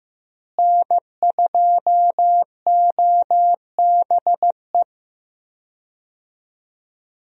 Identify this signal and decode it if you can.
Morse code, heard as a keyed tone, about 15 wpm, 700 Hz. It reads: N2OBE
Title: USA Call Signs - 5 Characters - Repeated 3x - 15wpm